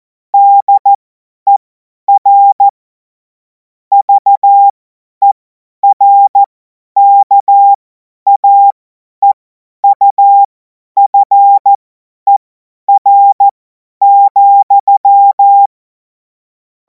Morse code, heard as a keyed tone, about 14 wpm, spaced as for 10 wpm, 800 Hz. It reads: DER VERKAEUFER,
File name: Morsekurs / Witze / Witz 17. Februar 2026 50 F70